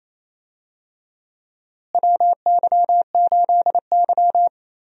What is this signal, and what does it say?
Morse code, heard as a keyed tone, about 28 wpm, 700 Hz. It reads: WY8Y
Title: USA Call Signs - 4 Characters - Repeated 3x - 28wpm